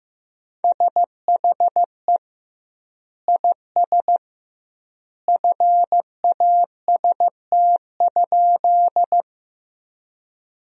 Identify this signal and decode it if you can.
Morse code, heard as a keyed tone, about 15 wpm, 700 Hz. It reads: SHE IS FAST?
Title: 2x Word-Spacing - Sentences from Top 400 Words - 15wpm